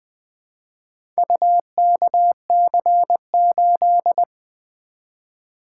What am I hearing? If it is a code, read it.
Morse code, heard as a keyed tone, about 20 wpm, 700 Hz. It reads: UKC8
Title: Four Letter-Number Combinations - 20wpm